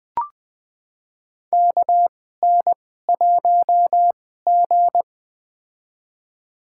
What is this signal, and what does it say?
Morse code, heard as a keyed tone, about 20 wpm, spaced as for 15 wpm, 700 Hz. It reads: KN1G
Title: USA Call Signs - 4 Characters - 2x Character Spacing - 20wpm